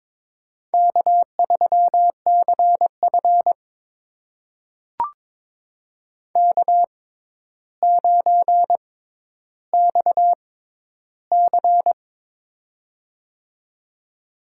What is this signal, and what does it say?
Morse code, heard as a keyed tone, about 21 wpm, spaced as for 11 wpm, 700 Hz. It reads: K3CF K9XC
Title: USA Call Signs - 4 Characters - 6x Character Spacing - 22wpm